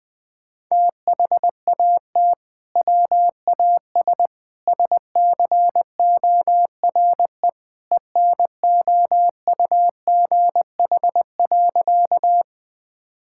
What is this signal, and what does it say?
Morse code, heard as a keyed tone, about 20 wpm, 700 Hz. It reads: THAT WAS SCORE ENOUGH.
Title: Sentences from Top 1600 Words 20wpm